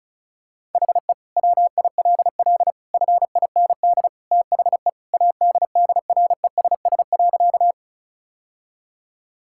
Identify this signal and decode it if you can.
Morse code, heard as a keyed tone, about 36 wpm, 700 Hz. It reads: HE WILL FIND THE ADDRESS.
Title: Sentences from Top 1000 Words - Review Part 6 - 35wpm